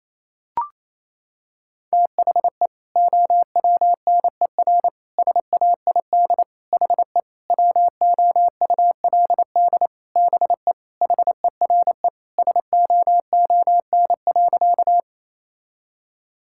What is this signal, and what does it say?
Morse code, heard as a keyed tone, about 28 wpm, 700 Hz. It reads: THE OWNER SAID HE WOULD BE HERE SOON.
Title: Sentences from Top 1600 Words 28wpm